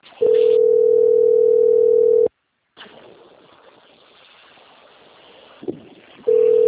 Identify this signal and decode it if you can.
Ringback tone (North America), 440 + 480 Hz.
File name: out- - - - - .6805